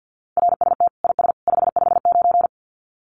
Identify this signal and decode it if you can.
Morse code, heard as a keyed tone, about 50 wpm, 700 Hz. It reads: RST IS 559
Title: Common QSO Phrases 50wpm